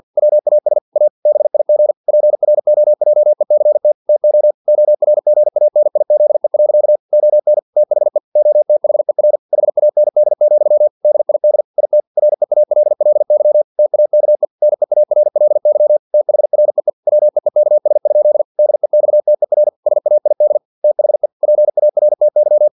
Morse code, heard as a keyed tone, about 50 wpm, 600 Hz. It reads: WAS A BIG PROJECT TO ORGANIZE. ON THE OTHER HAND, DID IT REALLY TAKE NEARLY THREE WEEKS? DEXTER SAID THE PARTY
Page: Never 49wpm